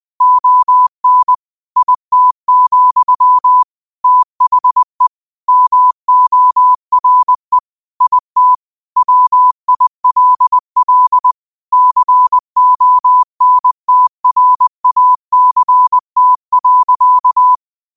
Morse code, heard as a keyed tone, about 20 wpm, 1000 Hz. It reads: ON IT, THE MORE IT WILL CONTRACT.